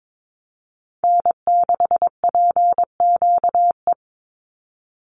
Morse code, heard as a keyed tone, about 22 wpm, 700 Hz. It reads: N6PQE
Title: USA Call Signs 22wpm